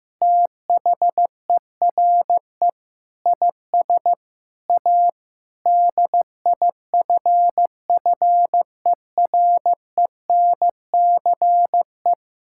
Morse code, heard as a keyed tone, about 15 wpm, 700 Hz. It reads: THERE IS A DIFFERENCE
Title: Sentences from Top 1000 Words - Review Part 5 - 15wpm